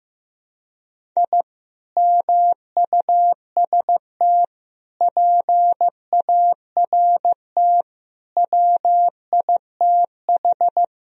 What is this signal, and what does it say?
Morse code, heard as a keyed tone, about 15 wpm, 700 Hz. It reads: I MUST PART WITH
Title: Sentences from Top 1000 Words - Review Part 1 - 15wpm